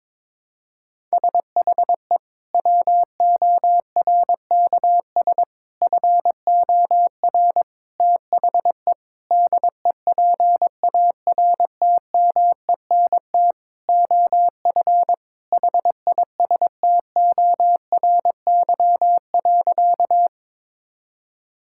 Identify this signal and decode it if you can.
Morse code, heard as a keyed tone, about 22 wpm, 700 Hz. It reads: SHE WORKS FOR THE DEPARTMENT OF HISTORY.
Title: Sentences from Top 1000 Words - Review Part 5 - 22wpm